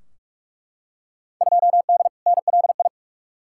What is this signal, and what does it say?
Morse code, heard as a keyed tone, about 42 wpm, 700 Hz. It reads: 2D NLI
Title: Field Day Exchanges Encore - 45wpm